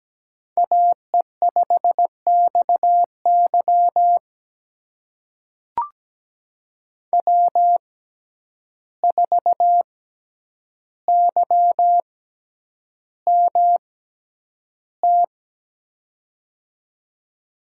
Morse code, heard as a keyed tone, about 17 wpm, spaced as for 9 wpm, 700 Hz. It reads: AE5XY W4YMT
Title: USA Call Signs - 5 Characters - 6x Character Spacing - 17wpm